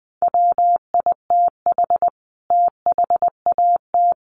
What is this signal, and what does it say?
Morse code, heard as a keyed tone, about 19 wpm, 700 Hz. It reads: WITH THAT
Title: Sentences from Top 1400 Words 20wpm